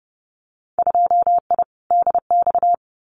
Morse code, heard as a keyed tone, about 29 wpm, 700 Hz. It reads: 2I DX
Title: Winter Field Day Exchanges 30wpm